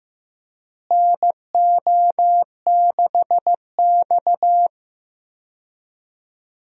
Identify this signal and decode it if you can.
Morse code, heard as a keyed tone, about 15 wpm, 700 Hz. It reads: NO6X
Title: USA Call Signs - 4 Characters - Repeated 3x - 15wpm